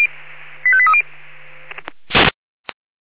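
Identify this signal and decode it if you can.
Five-tone selective call (ZVEI), heard as quick runs of five tones, tones of about 70 ms. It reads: 85830
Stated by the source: ZVEI five-tone selective call received on SDR.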